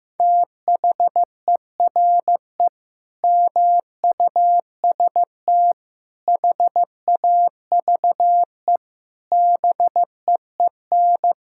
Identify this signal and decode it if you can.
Morse code, heard as a keyed tone, about 15 wpm, 700 Hz. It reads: THERE MUST HAVE BEEN
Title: Sentences from Top 1000 Words - Review Part 6 - 15wpm